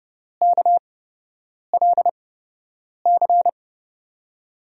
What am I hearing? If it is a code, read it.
Morse code, heard as a keyed tone, about 30 wpm, spaced as for 8 wpm, 700 Hz. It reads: KLC
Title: USA Call Signs - 6 Characters - 8x Character Spacing - 30wpm